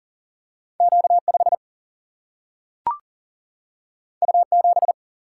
Morse code, heard as a keyed tone, about 40 wpm, 700 Hz. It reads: Q 5 U 7
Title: Single-Letter Single-Number 40wpm